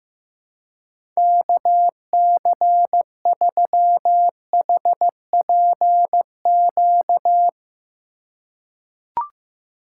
Morse code, heard as a keyed tone, about 15 wpm, 700 Hz. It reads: KC3HPQ
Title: USA Call Signs - 6 Characters - 4x Character Spacing - 15wpm